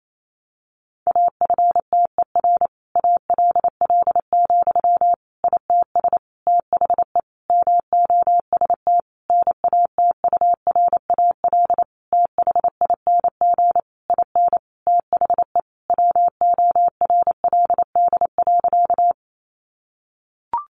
Morse code, heard as a keyed tone, about 28 wpm, 700 Hz. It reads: AFTER ALL, ITS THE MOST NATURAL THING IN THE WORLD.
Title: Sentences from Top 700 Words 28wpm